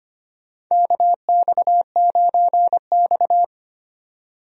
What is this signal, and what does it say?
Morse code, heard as a keyed tone, about 25 wpm, 700 Hz. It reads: KX9X
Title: USA Call Signs - 4 Characters - 25wpm